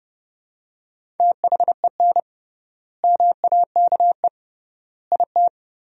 Morse code, heard as a keyed tone, about 30 wpm, 700 Hz. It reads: THEN MAKE IT
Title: 3x Word-Spacing - Sentences from Top 100 Words - 30wpm